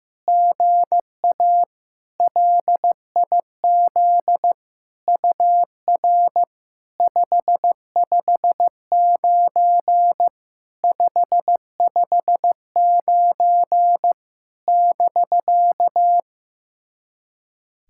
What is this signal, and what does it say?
Morse code, heard as a keyed tone, about 15 wpm, 700 Hz. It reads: GA LIZ UR 559 559 <BK>
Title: SOTA QSO Practice 15wpm